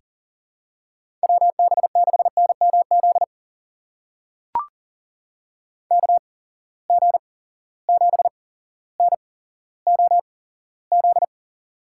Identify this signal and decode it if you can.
Morse code, heard as a keyed tone, about 37 wpm, spaced as for 17 wpm, 700 Hz. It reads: WB6NMZ KG7NOZ
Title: USA Call Signs - 6 Characters - 8x Character Spacing - 40wpm